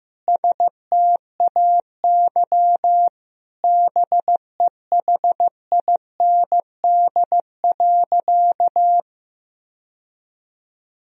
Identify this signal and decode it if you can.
Morse code, heard as a keyed tone, about 16 wpm, 700 Hz. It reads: STAY BEHIND.
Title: Sentences from Top 1000 Words - Review Part 1 - 15wpm